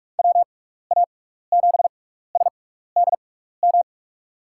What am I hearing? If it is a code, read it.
Morse code, heard as a keyed tone, about 44 wpm, spaced as for 15 wpm, 700 Hz. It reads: WA7SDM
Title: USA Call Signs - 6 Characters - 6x Character Spacing - 45wpm